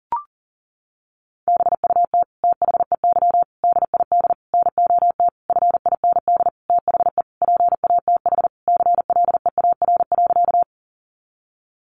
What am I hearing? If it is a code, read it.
Morse code, heard as a keyed tone, about 37 wpm, 700 Hz. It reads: BUT THEY DID NOT FIND THE PATH CLEAR.
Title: Sentences from Top 1400 Words 40wpm